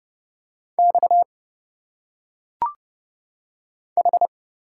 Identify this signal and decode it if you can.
Morse code, heard as a keyed tone, about 29 wpm, 700 Hz. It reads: X H
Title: Single Letters - Phonetic Pronunciation 30wpm